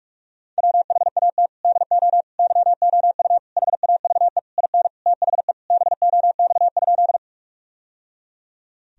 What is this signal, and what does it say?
Morse code, heard as a keyed tone, about 41 wpm, 700 Hz. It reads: WHAT DO YOU HAVE IN THE BOX?
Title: Sentences from Top 1000 Words - Review Part 2 - 45wpm